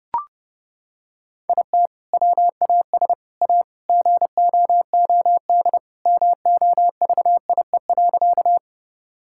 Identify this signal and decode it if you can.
Morse code, heard as a keyed tone, about 32 wpm, 700 Hz. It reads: IT WAS A GOOD MOVIE.
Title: Sentences from Top 1000 Words - Review Part 5 - 30wpm